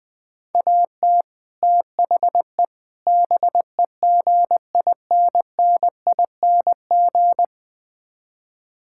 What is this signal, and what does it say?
Morse code, heard as a keyed tone, about 20 wpm, 700 Hz. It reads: AT THE BEGINNING
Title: Build A Phrase - 5-Word Phrases - 20wpm